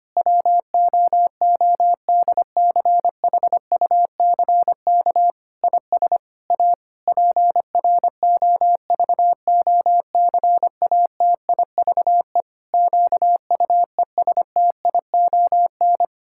Morse code, heard as a keyed tone, about 25 wpm, 700 Hz. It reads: WOODCHUCK IS A PROVOCATIVE QUESTION
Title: Pangram Challenge v2 - 25wpm